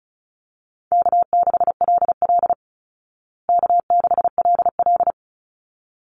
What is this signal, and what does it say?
Morse code, heard as a keyed tone, about 34 wpm, 700 Hz. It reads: K6LL K6LL
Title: Call Signs Repeated 3x - 35wpm